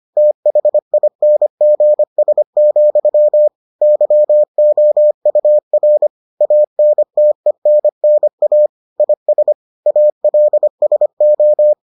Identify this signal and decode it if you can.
Morse code, heard as a keyed tone, about 25 wpm, 600 Hz. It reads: THINGS, YOUR ANTENNA IS ALSO